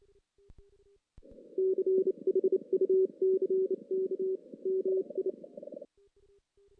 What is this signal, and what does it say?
Morse code, heard as a keyed tone, about 28 wpm, 380 Hz. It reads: CHUCK MI
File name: 2016 JAN CW NAQP / 2016 Jan naqp k5ot - 0-5